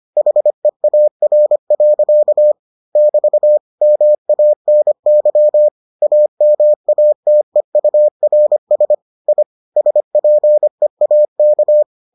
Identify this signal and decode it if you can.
Morse code, heard as a keyed tone, about 25 wpm, 600 Hz. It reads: HEAR. <BT> MANY AMATEURS I SPEAK